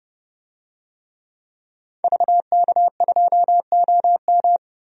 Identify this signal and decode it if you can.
Morse code, heard as a keyed tone, about 32 wpm, 700 Hz. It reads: VK2OM
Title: DXCC-100 Call Signs - 5 Characters - 30wpm